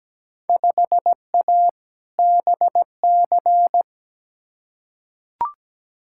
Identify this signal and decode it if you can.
Morse code, heard as a keyed tone, about 17 wpm, 700 Hz. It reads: 5A BC
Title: Field Day Exchanges - 17wpm